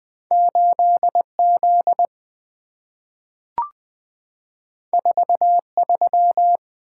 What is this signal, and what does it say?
Morse code, heard as a keyed tone, about 20 wpm, 700 Hz. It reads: 8 Z 4 3
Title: Two Letter-Number Combinations - 20wpm